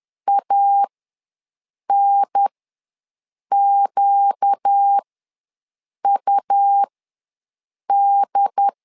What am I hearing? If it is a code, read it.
Morse code, heard as a keyed tone, about 10 wpm, spaced as for 6 wpm, 800 Hz. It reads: ANQUD
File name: LEZIONI / 4 Q ws 10wpm 2306